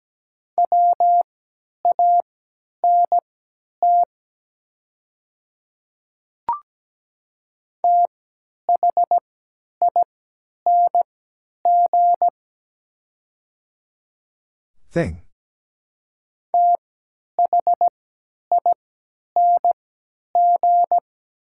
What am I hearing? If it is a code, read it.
Morse code, heard as a keyed tone, about 17 wpm, spaced as for 10 wpm, 700 Hz. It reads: WANT THING THING
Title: 3x Character Spacing - Top 200 Words - 17wpm